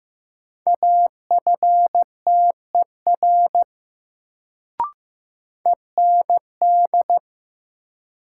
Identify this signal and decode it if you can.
Morse code, heard as a keyed tone, about 15 wpm, 700 Hz. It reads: AFTER END
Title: Sentences from Top 200 Words - New Word Drill - 15wpm